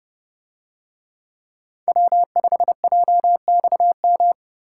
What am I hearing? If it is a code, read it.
Morse code, heard as a keyed tone, about 30 wpm, 700 Hz. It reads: W5JXM